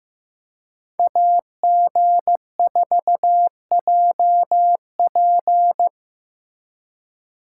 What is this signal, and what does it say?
Morse code, heard as a keyed tone, about 15 wpm, 700 Hz. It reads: AG4JP
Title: USA Call Signs - 3x Character Spacing - 15wpm